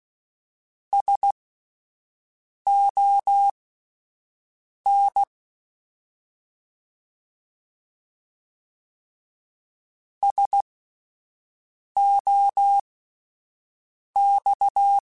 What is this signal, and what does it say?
Morse code, heard as a keyed tone, about 16 wpm, spaced as for 5 wpm, 800 Hz. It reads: SON SOX